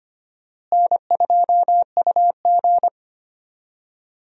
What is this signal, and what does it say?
Morse code, heard as a keyed tone, about 25 wpm, 700 Hz. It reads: N2UG
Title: USA Call Signs - 4 Characters - Repeated 3x - 25wpm